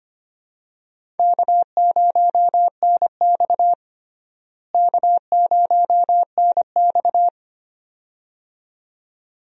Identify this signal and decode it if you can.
Morse code, heard as a keyed tone, about 25 wpm, 700 Hz. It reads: K0NX K0NX
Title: USA Call Signs - 4 Characters - Repeated 2x - 25wpm